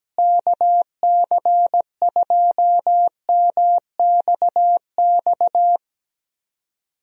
Morse code, heard as a keyed tone, about 17 wpm, 700 Hz. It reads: KC2MXX
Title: USA Call Signs - 6 Characters - Repeated 3x - 17wpm